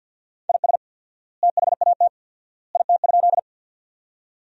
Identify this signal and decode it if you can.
Morse code, heard as a keyed tone, about 50 wpm, 700 Hz. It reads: IS THAT IT?
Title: 4x Word-Spacing - Sentences from Top 100 Words - 50wpm